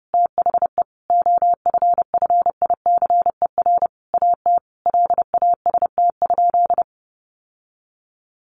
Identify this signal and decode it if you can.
Morse code, heard as a keyed tone, about 30 wpm, 700 Hz. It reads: THE OFFICER AT LAST?
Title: Sentences from Top 1300 Words 30wpm